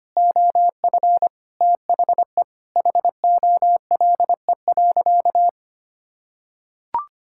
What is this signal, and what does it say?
Morse code, heard as a keyed tone, about 25 wpm, 700 Hz. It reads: OF THE HOLE.
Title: Sentences from Top 1600 Words 25wpm